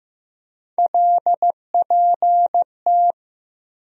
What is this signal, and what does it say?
Morse code, heard as a keyed tone, about 15 wpm, 700 Hz. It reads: LPT
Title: USA Call Sign Postfixes - 3 Letters 15wpm